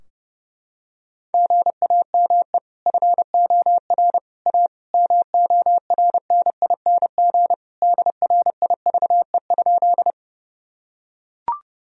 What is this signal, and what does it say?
Morse code, heard as a keyed tone, about 30 wpm, 700 Hz. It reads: GAME FOR A MORNING DRIVE?
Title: Sentences from Top 1000 Words - Review Part 3 - 30wpm